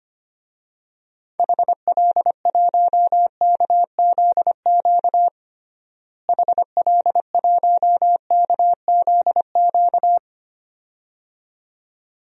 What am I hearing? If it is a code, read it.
Morse code, heard as a keyed tone, about 25 wpm, 700 Hz. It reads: HL1KZQ HL1KZQ
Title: DXCC-100 Call Signs - 6 Characters - Repeated 2x - 25wpm